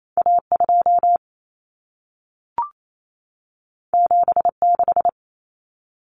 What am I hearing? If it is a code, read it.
Morse code, heard as a keyed tone, about 28 wpm, 700 Hz. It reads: A2 76
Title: Common Cut Numbers - 2-Digit 28wpm